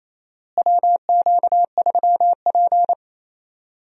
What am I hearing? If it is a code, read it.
Morse code, heard as a keyed tone, about 28 wpm, 700 Hz. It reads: WQ3P
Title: USA Call Signs - 4 Characters - Repeated 3x - 28wpm